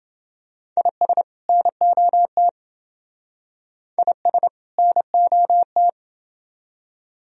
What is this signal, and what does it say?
Morse code, heard as a keyed tone, about 28 wpm, 700 Hz. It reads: IS NOT IS NOT